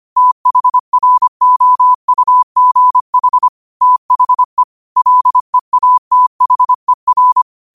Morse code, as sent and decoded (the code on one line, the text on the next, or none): - .... .-. --- ..- --. .... / - .... / .-.. .- - .... .-.
THROUGH THE LEATHER